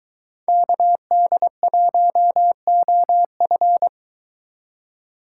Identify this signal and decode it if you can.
Morse code, heard as a keyed tone, about 22 wpm, 700 Hz. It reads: KD1OF